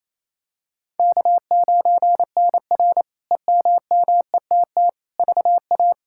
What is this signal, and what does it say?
Morse code, heard as a keyed tone, about 28 wpm, 700 Hz. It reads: K9NR EMMETT VA
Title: K1USN SST QSO Practice - 28wpm